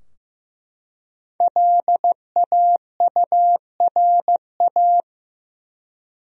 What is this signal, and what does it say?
Morse code, heard as a keyed tone, about 15 wpm, 700 Hz. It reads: LAURA
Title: USA Common Names - Max 5 Letters - 15wpm